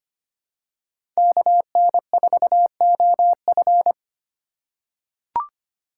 Morse code, heard as a keyed tone, about 25 wpm, 700 Hz. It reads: KN4OF
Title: USA Call Signs - 3x Character Spacing - 25wpm